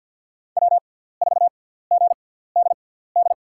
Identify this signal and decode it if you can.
Morse code, heard as a keyed tone, about 49 wpm, spaced as for 17 wpm, 700 Hz. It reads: W4GDD
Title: USA Call Signs - 5 Characters - 6x Character Spacing - 50wpm